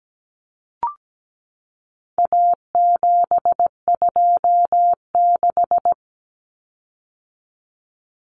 Morse code, heard as a keyed tone, about 17 wpm, 700 Hz. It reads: A726
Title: Common Cut Numbers - 4-Digit 17wpm